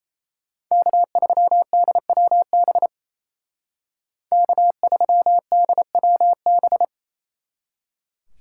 Morse code, heard as a keyed tone, about 37 wpm, 700 Hz. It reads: K3DWB K3DWB